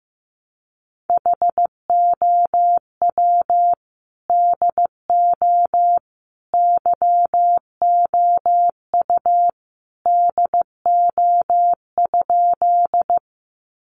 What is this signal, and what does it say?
Morse code, heard as a keyed tone, about 15 wpm, 700 Hz. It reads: HOW DO YOU DO?
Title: Sentences from Top 100 Words 15wpm